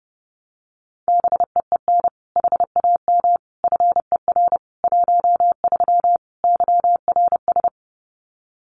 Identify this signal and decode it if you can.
Morse code, heard as a keyed tone, about 30 wpm, 700 Hz. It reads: BEEN HAM FER 13 YRS
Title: Common QSO Phrases 30wpm